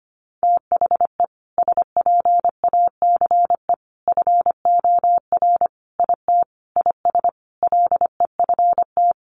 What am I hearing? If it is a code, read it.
Morse code, heard as a keyed tone, about 25 wpm, 700 Hz. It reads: THE SPACE FOR IT IS LEFT
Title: Sentences from Top 1500 Words 25wpm